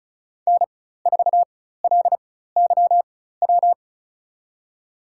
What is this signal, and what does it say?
Morse code, heard as a keyed tone, about 36 wpm, spaced as for 16 wpm, 700 Hz. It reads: N4LYW
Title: USA Call Signs - 5 Characters - 4x Character Spacing - 35wpm